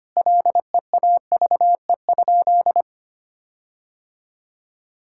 Morse code, heard as a keyed tone, about 25 wpm, 700 Hz. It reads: LEAVE?
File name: practice / 25wpm / Sentences with 4-Word Phrases - 25wpm